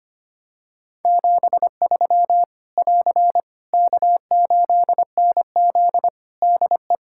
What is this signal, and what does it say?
Morse code, heard as a keyed tone, about 25 wpm, 700 Hz. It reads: 73 <AR> K8NZ DE
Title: Practice QSO - Exchange 4 - Ending - 25wpm